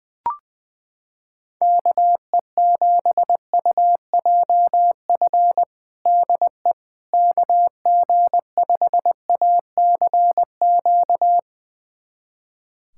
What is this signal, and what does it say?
Morse code, heard as a keyed tone, about 21 wpm, 700 Hz. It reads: KE7UJF DE KG5ACQ
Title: USA Callsign DE Callsign - 6 Characters - 20wpm